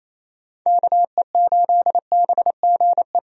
Morse code, heard as a keyed tone, about 28 wpm, 700 Hz. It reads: KE8BGE
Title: USA Call Signs - 6 Characters - 28wpm